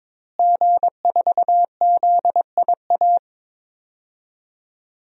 Morse code, heard as a keyed tone, about 22 wpm, 700 Hz. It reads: G4ZIA
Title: DXCC-100 Call Signs - 5 Characters - Repeated 3x - 22wpm